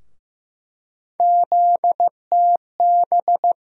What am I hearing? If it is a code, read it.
Morse code, heard as a keyed tone, about 15 wpm, 700 Hz. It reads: ZTB
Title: USA Call Sign Postfixes - 3 Letters 15wpm